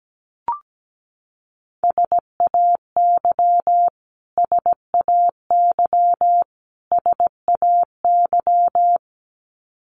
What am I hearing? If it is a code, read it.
Morse code, heard as a keyed tone, about 17 wpm, 700 Hz. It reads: SAY SAY SAY
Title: Top 200 Words Repeated 17wpm